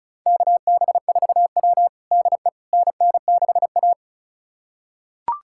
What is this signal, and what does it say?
Morse code, heard as a keyed tone, about 34 wpm, 700 Hz. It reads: KB4W DE NN6A
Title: USA Callsign DE Callsign - 4 Characters - 35wpm